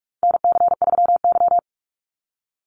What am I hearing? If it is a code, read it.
Morse code, heard as a keyed tone, about 44 wpm, 700 Hz. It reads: NC3Y
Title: Call Signs Repeated 3x - 45wpm